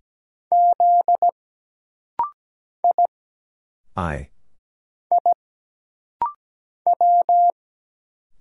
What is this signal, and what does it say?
Morse code, heard as a keyed tone, about 17 wpm, 700 Hz. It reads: Z I I W
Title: Single Letter-Number Alt - Rapid-Fire 17wpm